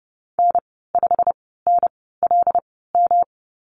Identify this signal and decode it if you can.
Morse code, heard as a keyed tone, about 30 wpm, spaced as for 17 wpm, 700 Hz. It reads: N5NLM
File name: Lesson 19g - Letter w - Call Signs 30wpm